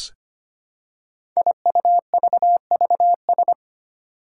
Morse code, heard as a keyed tone, about 25 wpm, 700 Hz. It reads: IUVVS
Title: Five Challenging Letter-Number Combinations 25wpm